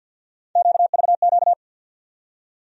Morse code, heard as a keyed tone, about 52 wpm, 700 Hz. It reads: QVQ
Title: USA Call Sign Postfixes - 3 Letters 50wpm